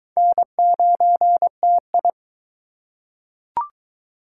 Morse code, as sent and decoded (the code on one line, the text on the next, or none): -. ----. - ..
N9TI